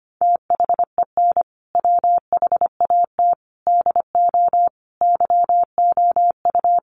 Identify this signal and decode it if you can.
Morse code, heard as a keyed tone, about 25 wpm, 700 Hz. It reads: THEN WHAT DO YOU
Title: Sentences from Top 1300 Words 25wpm